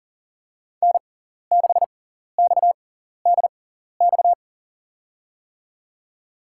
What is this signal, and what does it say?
Morse code, heard as a keyed tone, about 40 wpm, spaced as for 14 wpm, 700 Hz. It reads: N6XDX
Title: USA Call Signs - 5 Characters - 6x Character Spacing - 40wpm